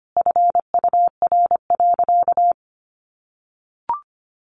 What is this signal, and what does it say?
Morse code, heard as a keyed tone, about 25 wpm, 700 Hz. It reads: FUR.
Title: Easy Phrases 25wpm